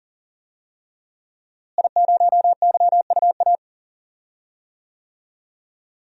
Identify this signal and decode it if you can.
Morse code, heard as a keyed tone, about 40 wpm, 700 Hz. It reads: I0YUA
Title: DXCC-100 Call Signs - 5 Characters - Repeated 3x - 40wpm